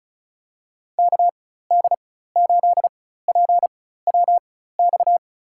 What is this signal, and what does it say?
Morse code, heard as a keyed tone, about 36 wpm, spaced as for 16 wpm, 700 Hz. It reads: KD8PWX
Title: USA Call Signs - 6 Characters - 4x Character Spacing - 35wpm